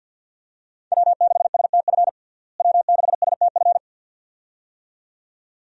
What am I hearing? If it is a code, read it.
Morse code, heard as a keyed tone, about 49 wpm, 700 Hz. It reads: W6STF W6STF
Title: USA Call Signs - 5 Characters - Repeated 2x - 50wpm